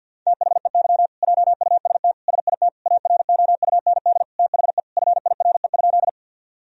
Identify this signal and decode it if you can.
Morse code, heard as a keyed tone, about 50 wpm, 700 Hz. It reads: THEY JUST SIT AROUND THE FIRE?
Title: Sentences from Top 1000 Words - Review Part 3 - 50wpm